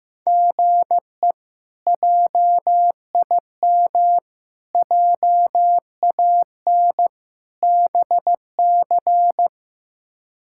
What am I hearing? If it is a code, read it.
Morse code, heard as a keyed tone, about 15 wpm, 700 Hz. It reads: GE JIM JAN BC
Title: K1USN SST QSO Practice - 15wpm